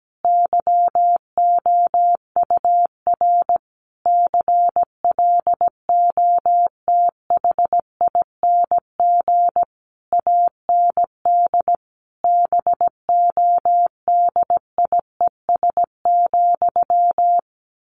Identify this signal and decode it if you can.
Morse code, heard as a keyed tone, about 17 wpm, 700 Hz. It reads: YOUR CLOTHING AND BODIES,
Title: A Sound Of Thunder 17wpm